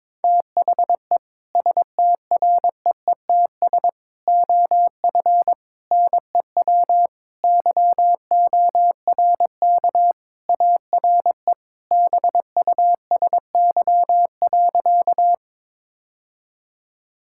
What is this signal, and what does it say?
Morse code, heard as a keyed tone, about 22 wpm, 700 Hz. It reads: THE STREETS OF NEW YORK ARE BUSY.
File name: practice / 22wpm / Sentences with 3-Word Phrases - 22wpm